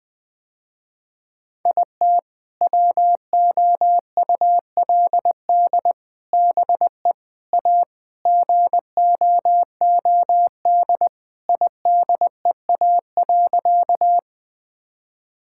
Morse code, heard as a keyed tone, about 20 wpm, 700 Hz. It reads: IT WOULD BE A GOOD IDEA.
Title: Sentences with 3-Word Phrases - Spoken Beginning - 20wpm